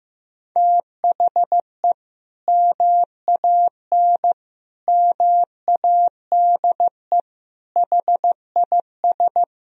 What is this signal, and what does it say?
Morse code, heard as a keyed tone, about 15 wpm, 700 Hz. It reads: THE MAN MADE HIS
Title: Sentences from Top 1000 Words - Review Part 2 - 15wpm